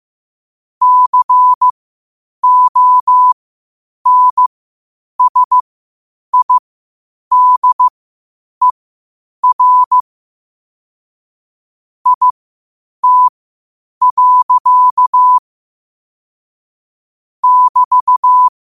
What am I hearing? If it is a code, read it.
Morse code, heard as a keyed tone, about 15 wpm, spaced as for 8 wpm, 1000 Hz. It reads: CONSIDER IT. <BT>
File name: qotdfiles / 8wpm